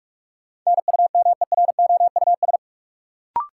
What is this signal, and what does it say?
Morse code, heard as a keyed tone, about 46 wpm, 700 Hz. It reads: NUMEROUS
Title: Sentences from Top 1400 Words - New Word Drill Alt - 45wpm